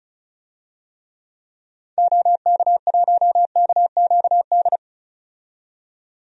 Morse code, heard as a keyed tone, about 34 wpm, 700 Hz. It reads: OK1KQD
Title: DXCC-100 Call Signs - 6 Characters - 35wpm